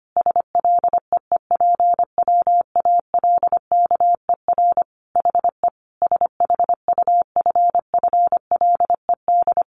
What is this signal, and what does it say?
Morse code, heard as a keyed tone, about 25 wpm, 700 Hz. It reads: SLEEPWALKER HE SHUFFLED
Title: A Sound Of Thunder 25wpm